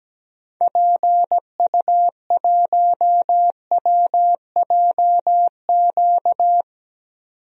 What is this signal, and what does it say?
Morse code, heard as a keyed tone, about 17 wpm, 700 Hz. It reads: PU1WJQ
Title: DXCC-100 Call Signs - 6 Characters - 17wpm